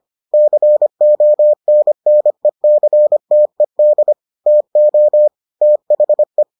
Morse code, heard as a keyed tone, about 25 wpm, 600 Hz. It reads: CONNECTED TO THE